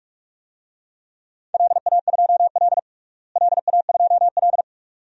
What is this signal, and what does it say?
Morse code, heard as a keyed tone, about 46 wpm, 700 Hz. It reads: LA2L LA2L
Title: DXCC-100 Call Signs - 4 Characters - Repeated 2x - 45wpm